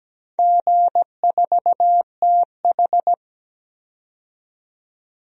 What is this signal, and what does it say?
Morse code, heard as a keyed tone, about 17 wpm, 700 Hz. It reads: G4TH
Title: DXCC-100 Call Signs - 4 Characters - 17wpm